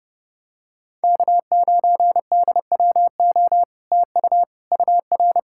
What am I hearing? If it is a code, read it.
Morse code, heard as a keyed tone, about 30 wpm, 700 Hz. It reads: K9DWO TU UR